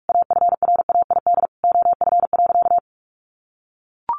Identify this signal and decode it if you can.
Morse code, heard as a keyed tone, about 44 wpm, 700 Hz. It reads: AFRAID OF.
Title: Sentences from Top 1500 Words 45wpm